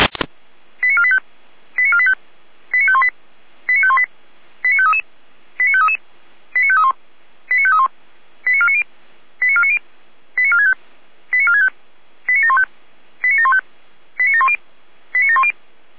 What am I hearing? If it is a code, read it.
Five-tone selective call (ZVEI), heard as quick runs of five tones, tones of about 70 ms. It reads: 89485 89485 89518 89518 89533 89533 89531 89531 89490 89490 89465 89465 89715 89715 89710 89710